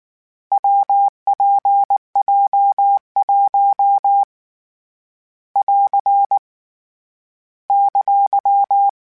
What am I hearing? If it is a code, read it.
Morse code, heard as a keyed tone, about 19 wpm, 800 Hz. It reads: WPJ1 <AR> !
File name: Difficult Combos / Difficult Combos-mp3s / WPJ WPJ1 @18